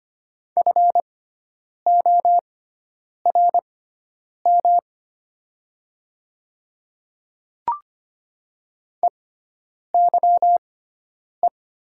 Morse code, heard as a keyed tone, about 25 wpm, spaced as for 9 wpm, 700 Hz. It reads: FORM EYE
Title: 6x Character Spacing - Top 500 Words - 25wpm